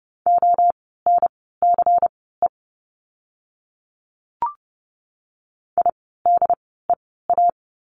Morse code, heard as a keyed tone, about 30 wpm, spaced as for 17 wpm, 700 Hz. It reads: ONCE IDEA
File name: course / 30:10wpm / 108 Lesson 18f - Letter m - Words 30wpm